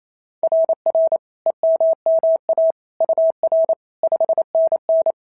Morse code, heard as a keyed tone, about 28 wpm, 690 Hz. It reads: RR EMMA UR 5NN
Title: SOTA QSO Practice 28wpm